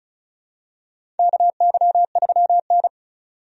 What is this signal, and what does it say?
Morse code, heard as a keyed tone, about 34 wpm, 700 Hz. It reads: KY3N
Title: USA Call Signs - 4 Characters - 35wpm